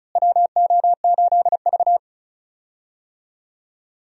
Morse code, heard as a keyed tone, about 36 wpm, 700 Hz. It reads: WO8V
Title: USA Call Signs - 4 Characters - Repeated 2x - 35wpm